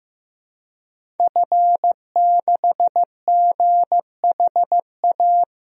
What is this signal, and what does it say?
Morse code, heard as a keyed tone, about 15 wpm, 700 Hz. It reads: F6GHA
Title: DXCC-100 Call Signs - 5 Characters - Repeated 2x - 15wpm